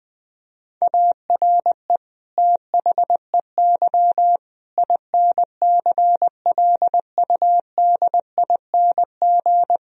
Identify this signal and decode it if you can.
Morse code, heard as a keyed tone, about 21 wpm, 700 Hz. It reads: ARE THEY INCLUDING
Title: Sentences from Top 1000 Words - Review Part 4 - 20wpm